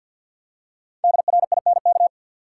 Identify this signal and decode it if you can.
Morse code, heard as a keyed tone, about 52 wpm, 700 Hz. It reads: DRINK
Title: Five Letter Words - 50wpm